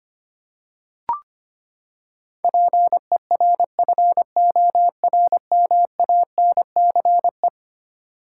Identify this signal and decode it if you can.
Morse code, heard as a keyed tone, about 25 wpm, 700 Hz. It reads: PERFORMANCE
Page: morse-code-ninja-practice-25wpm